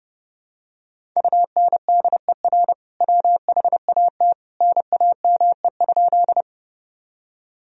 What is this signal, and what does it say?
Morse code, heard as a keyed tone, about 32 wpm, 700 Hz. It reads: UNDER WHAT NAME?
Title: Sentences from Top 1000 Words - Review Part 1 - 30wpm